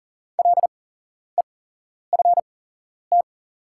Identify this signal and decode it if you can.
Morse code, heard as a keyed tone, about 39 wpm, spaced as for 11 wpm, 690 Hz. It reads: LEFT
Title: 8x Character Spacing - Top 300 Words - 40wpm